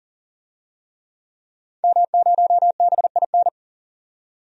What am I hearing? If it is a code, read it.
Morse code, heard as a keyed tone, about 40 wpm, 700 Hz. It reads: M0BIN
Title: DXCC-100 Call Signs - 5 Characters - 40wpm